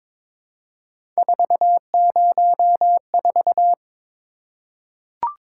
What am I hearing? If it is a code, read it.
Morse code, heard as a keyed tone, about 22 wpm, 700 Hz. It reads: 404
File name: practice / 22wpm / Three Letter-Number Combinations - 22wpm